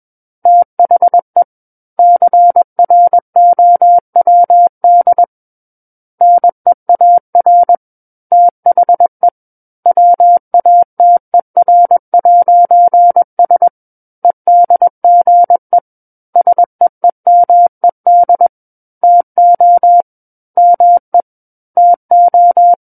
Morse code, heard as a keyed tone, about 21 wpm, 700 Hz. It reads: THE CROWD NEAR THE WATER'S EDGE SEEMED TO ME TO
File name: Book One / 12-Chapter 12 - 21 WPM